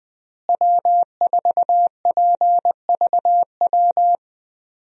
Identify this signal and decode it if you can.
Morse code, heard as a keyed tone, about 20 wpm, 700 Hz. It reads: W4PVW
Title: USA Call Signs - 5 Characters - Repeated 3x - 20wpm